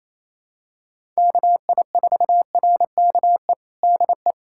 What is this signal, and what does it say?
Morse code, heard as a keyed tone, about 28 wpm, 700 Hz. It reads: KI4RKE DE
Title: USA Callsign DE Callsign - 6 Characters - 28wpm